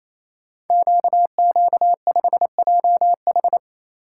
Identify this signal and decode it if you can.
Morse code, heard as a keyed tone, about 27 wpm, 700 Hz. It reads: QQ5JH
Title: Code Groups - 5 Letter:Numbers - 28wpm